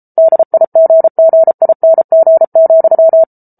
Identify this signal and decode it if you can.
Morse code, heard as a keyed tone, about 34 wpm, 650 Hz. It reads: DIGGING,